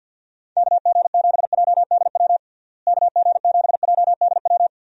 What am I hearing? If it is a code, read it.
Morse code, heard as a keyed tone, about 52 wpm, 700 Hz. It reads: KG7JDW KG7JDW
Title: USA Call Signs - 6 Characters - Repeated 2x - 50wpm